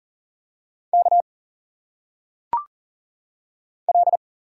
Morse code, heard as a keyed tone, about 40 wpm, 700 Hz. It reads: K L
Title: Single Letters - Phonetic Pronunciation 40wpm